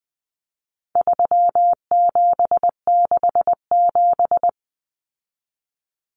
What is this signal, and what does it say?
Morse code, heard as a keyed tone, about 20 wpm, 700 Hz. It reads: 3767
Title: Numbers - 4-Digit 20wpm